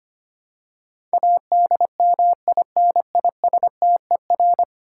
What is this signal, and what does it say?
Morse code, heard as a keyed tone, about 25 wpm, 700 Hz. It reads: ADMINISTER
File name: Ten Letter Words - 25wpm